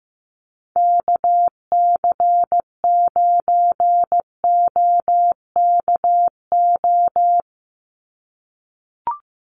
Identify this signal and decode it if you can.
Morse code, heard as a keyed tone, about 15 wpm, 700 Hz. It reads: KC9OKO
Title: USA Call Signs 15wpm